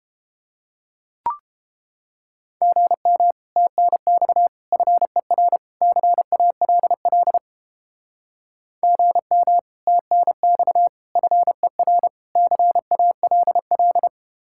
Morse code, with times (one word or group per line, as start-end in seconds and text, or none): 2.6-3.3 GM
3.6-4.5 TNX
4.7-5.6 FER
5.8-7.4 CALL
8.8-9.6 GM
9.9-10.9 TNX
11.2-12.1 FER
12.4-14.1 CALL